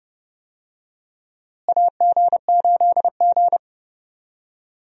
Morse code, heard as a keyed tone, about 30 wpm, 700 Hz. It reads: AG8G
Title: USA Call Signs - 4 Characters - Repeated 3x - 30wpm